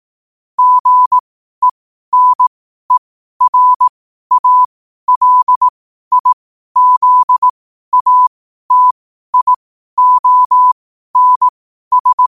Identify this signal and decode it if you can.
Morse code, heard as a keyed tone, about 18 wpm, spaced as for 12 wpm, 1000 Hz. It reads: GENERALIZATIONS
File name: 12wpm